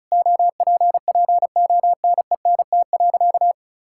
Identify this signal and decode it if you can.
Morse code, heard as a keyed tone, about 34 wpm, 700 Hz. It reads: OPPONENT.
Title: Pangram Challenge v2 - 35wpm